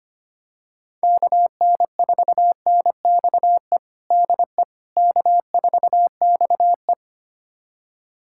Morse code, heard as a keyed tone, about 25 wpm, 700 Hz. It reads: KN4NXE DE K4XE
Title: USA Callsign DE Callsign 25wpm